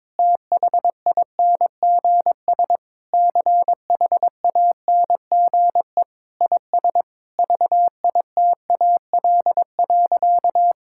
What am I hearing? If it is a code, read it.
Morse code, heard as a keyed tone, about 22 wpm, 700 Hz. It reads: THINGS CHANGE IS VITAL.
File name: practice / 22wpm / Sentences with 8-Word Phrases - 22wpm